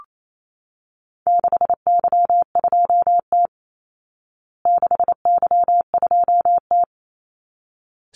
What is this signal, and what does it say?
Morse code, heard as a keyed tone, about 28 wpm, 700 Hz. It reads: 6Y2T 6Y2T
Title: Call Signs Repeated 3x - 28wpm